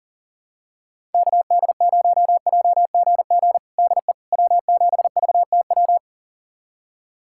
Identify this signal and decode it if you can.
Morse code, heard as a keyed tone, about 40 wpm, 700 Hz. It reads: KD0JGG DE W7VTW
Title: USA Callsign DE Callsign 40wpm